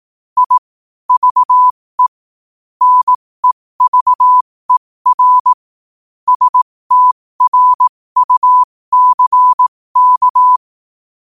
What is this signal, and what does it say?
Morse code, heard as a keyed tone, about 18 wpm, spaced as for 15 wpm, 1000 Hz. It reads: I VE NEVER STRUCK
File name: qotdfiles / 15wpm